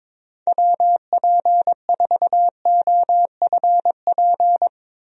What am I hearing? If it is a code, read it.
Morse code, heard as a keyed tone, about 22 wpm, 700 Hz. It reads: WP4OFP
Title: USA Call Signs - 6 Characters - Repeated 2x - 22wpm